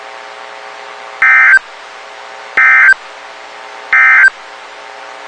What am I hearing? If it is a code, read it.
Emergency Alert System (SAME) tones, heard as harsh buzzing bursts. It is heard from 1.1 to 4.4 s.
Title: WXK91 NOAA Weather Radio